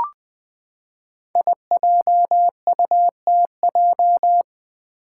Morse code, heard as a keyed tone, about 20 wpm, 700 Hz. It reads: IJUTJ
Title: Five Challenging Letter-Number Combinations 20wpm